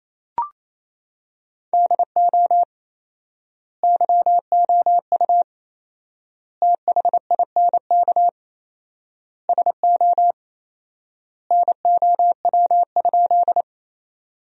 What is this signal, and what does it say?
Morse code, heard as a keyed tone, about 28 wpm, 700 Hz. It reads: DO YOU THINK SO NOW?
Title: 4x Word-Spacing - Sentences from Top 200 Words - 28wpm